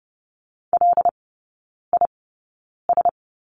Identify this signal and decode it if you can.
Morse code, heard as a keyed tone, about 30 wpm, spaced as for 9 wpm, 700 Hz. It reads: LIS
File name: 081 Lesson 14e - Number 2 - Three Characters 30wpm